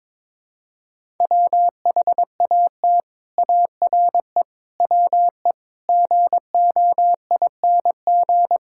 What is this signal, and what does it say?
Morse code, heard as a keyed tone, about 22 wpm, 700 Hz. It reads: WHAT ARE WE GOING
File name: practice / 22wpm / Sets of 8 Words - 22wpm